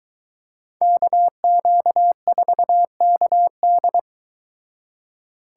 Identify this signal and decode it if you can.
Morse code, heard as a keyed tone, about 23 wpm, 700 Hz. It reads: KQ4KD